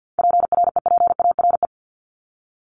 Morse code, heard as a keyed tone, about 50 wpm, 700 Hz. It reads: PREPARE
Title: Top 751-1000 Words 50wpm